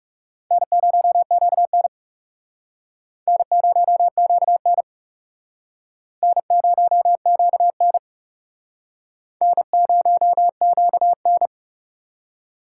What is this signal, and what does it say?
Morse code, heard as a keyed tone, about 37 wpm, 700 Hz. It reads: N0QN N0QN N0QN N0QN
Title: Speed Racing - USA 1x2 Call Signs 30wpm